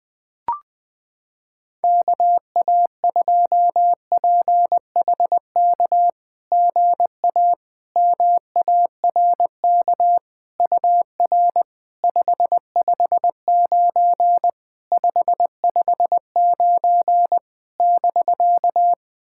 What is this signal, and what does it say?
Morse code, heard as a keyed tone, about 20 wpm, 690 Hz. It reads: KA2PHK GA MARK UR 559 559 <BK>
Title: SOTA QSO Practice 20wpm